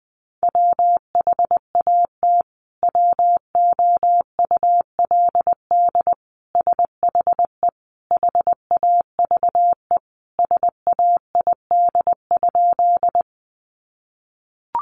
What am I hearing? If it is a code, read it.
Morse code, heard as a keyed tone, about 20 wpm, 700 Hz. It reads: WHAT WOULD SHE HAVE SAID?
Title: Sentences from Top 100 Words 20wpm